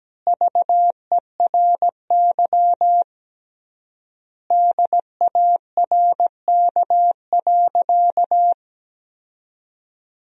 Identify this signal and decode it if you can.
Morse code, heard as a keyed tone, about 17 wpm, 700 Hz. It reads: VERY DARK.
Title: 3x Word-Spacing - Sentences from Top 500 Words - 17wpm